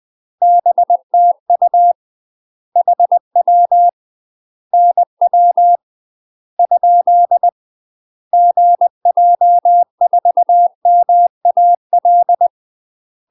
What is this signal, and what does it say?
Morse code, heard as a keyed tone, about 20 wpm, 700 Hz. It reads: BTU HW NW ? GJ4MAL
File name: UK MASTER QSOs TRAINING / 20WPM / UK MASTER QSOs 0039